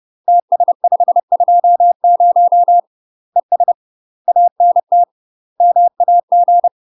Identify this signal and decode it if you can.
Morse code, heard as a keyed tone, about 30 wpm, 700 Hz. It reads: TS520 ES ANT MAG
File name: UK MASTER QSOs TRAINING / 30WPM / UK MASTER QSOs 0008